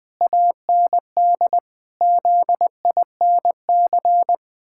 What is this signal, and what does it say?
Morse code, heard as a keyed tone, about 20 wpm, 700 Hz. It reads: AND ZINC
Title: Pangram Challenge v2 - 20wpm